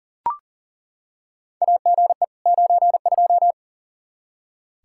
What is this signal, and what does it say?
Morse code, heard as a keyed tone, about 40 wpm, 700 Hz. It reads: AGE 92
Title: Practice QSO - Exchange 3 - More Info - 40wpm